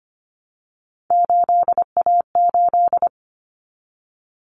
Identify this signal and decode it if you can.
Morse code, heard as a keyed tone, about 25 wpm, 700 Hz. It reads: 8A8
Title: Common Cut Numbers - 3-Digit 25wpm